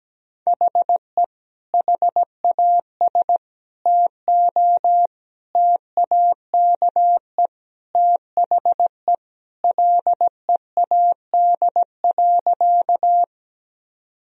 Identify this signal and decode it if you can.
Morse code, heard as a keyed tone, about 17 wpm, 700 Hz. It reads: HE HAS TO TAKE THE LEAD.
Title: Sentences from Top 1000 Words - Review Part 2 - 17wpm